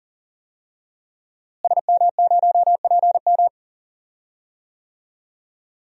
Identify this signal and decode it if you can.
Morse code, heard as a keyed tone, about 40 wpm, 700 Hz. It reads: SM0PM
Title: DXCC-100 Call Signs - 5 Characters - Repeated 2x - 40wpm